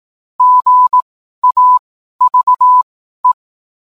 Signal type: Morse code